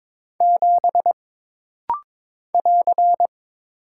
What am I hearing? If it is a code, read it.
Morse code, heard as a keyed tone, about 22 wpm, 700 Hz. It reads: 7 <AR>
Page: This rapid-fire timing is geared to push your ICR (Instant Character Recognition) skills.